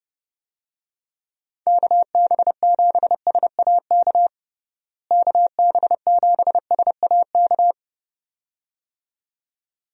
Morse code, heard as a keyed tone, about 30 wpm, 700 Hz. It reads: KB7SAK KB7SAK
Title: USA Call Signs - 6 Characters - Repeated 2x - 30wpm